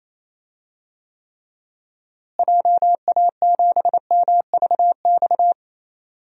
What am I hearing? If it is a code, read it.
Morse code, heard as a keyed tone, about 27 wpm, 700 Hz. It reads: JA7MVX
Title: DXCC-100 Call Signs - 6 Characters - Repeated 3x - 28wpm